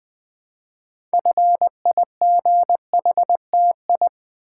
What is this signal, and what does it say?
Morse code, heard as a keyed tone, about 20 wpm, 700 Hz. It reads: FIGHTI
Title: Build A Word - 3 Letters At A Time - 8-Letter Words - 20wpm